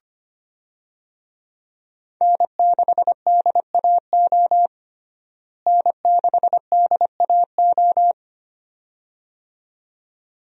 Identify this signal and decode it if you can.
Morse code, heard as a keyed tone, about 25 wpm, 700 Hz. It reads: N6DAO N6DAO
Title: USA Call Signs - 5 Characters - Repeated 2x - 25wpm